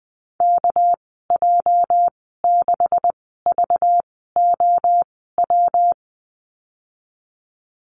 Morse code, heard as a keyed tone, about 20 wpm, spaced as for 14 wpm, 700 Hz. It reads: KJ6VOW
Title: 245 Lesson 39g - Letter z - Call Signs 20wpm